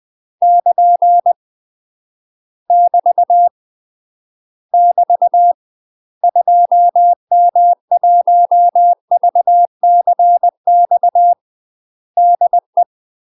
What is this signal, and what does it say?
Morse code, heard as a keyed tone, about 20 wpm, 700 Hz. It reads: <KN> <BT> <BT> 2M1VCX DE
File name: UK MASTER QSOs TRAINING / 20WPM / UK MASTER QSOs 0018